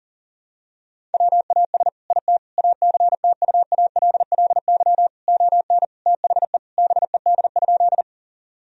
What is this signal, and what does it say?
Morse code, heard as a keyed tone, about 37 wpm, 700 Hz. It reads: WAS IT ACTUALLY ON THE BED?
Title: Sentences from Top 1000 Words - Review Part 4 - 40wpm